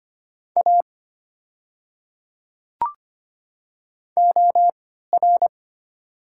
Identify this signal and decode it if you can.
Morse code, heard as a keyed tone, about 25 wpm, spaced as for 14 wpm, 690 Hz. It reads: A OR
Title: 3x Character Spacing - Top 200 Words - 25wpm